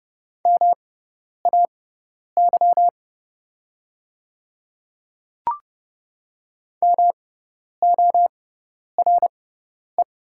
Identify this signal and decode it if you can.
Morse code, heard as a keyed tone, about 30 wpm, spaced as for 10 wpm, 690 Hz. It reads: MAY MORE
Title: 6x Character Spacing - Top 200 Words - 30wpm